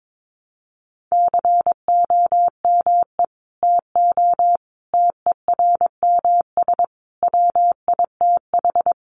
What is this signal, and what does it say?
Morse code, heard as a keyed tone, about 22 wpm, 700 Hz. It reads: COME TO TERMS WITH
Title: Sets of 5 Words Encore2 22wpm